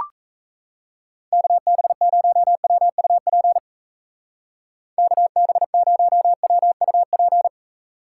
Morse code, heard as a keyed tone, about 36 wpm, 700 Hz. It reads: KB0WUP KB0WUP